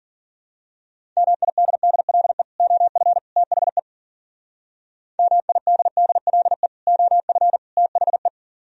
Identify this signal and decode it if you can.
Morse code, heard as a keyed tone, about 43 wpm, 700 Hz. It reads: MIDDLE OF THE MIDDLE OF THE
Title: Speed-Racing - Sets of 3 Words 40wpm